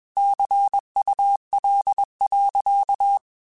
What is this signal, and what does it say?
Morse code, heard as a keyed tone, about 21 wpm, 800 Hz. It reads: CUL.
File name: QSOs / QSOs-mp3s / WA7QOX 2@20